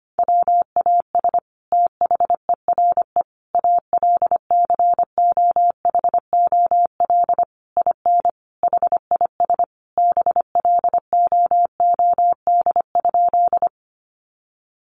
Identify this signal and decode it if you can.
Morse code, heard as a keyed tone, about 25 wpm, 700 Hz. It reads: WAS THERE ALCOHOL IN HIS BLOOD?
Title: Sentences from Top 1400 Words 25wpm